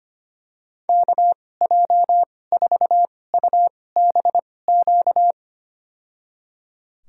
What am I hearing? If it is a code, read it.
Morse code, heard as a keyed tone, about 25 wpm, spaced as for 18 wpm, 700 Hz. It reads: KJ4UBQ
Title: USA Call Signs - 2x Character Spacing - 25wpm